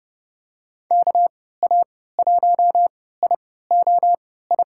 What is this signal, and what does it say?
Morse code, heard as a keyed tone, about 30 wpm, spaced as for 17 wpm, 700 Hz. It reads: KA1IOI
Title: USA Call Signs - 6 Characters - 3x Character Spacing - 30wpm